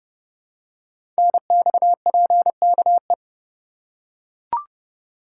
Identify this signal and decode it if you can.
Morse code, heard as a keyed tone, about 30 wpm, 700 Hz. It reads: NXPKE
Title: Code Groups - 5 Letters - 30wpm